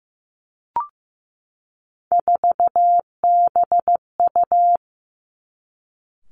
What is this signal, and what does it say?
Morse code, heard as a keyed tone, about 15 wpm, 700 Hz. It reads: 4BU
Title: Three Challenging Letter-Number Combinations 15wpm